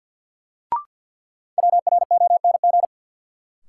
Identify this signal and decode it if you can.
Morse code, heard as a keyed tone, about 53 wpm, 700 Hz. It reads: WRONG
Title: Sentences from Top 900 Words - New Word Drill - 50wpm